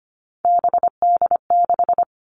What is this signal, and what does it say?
Morse code, heard as a keyed tone, about 25 wpm, 700 Hz. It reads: BD6
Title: All Cut Numbers - 3-Digit 25wpm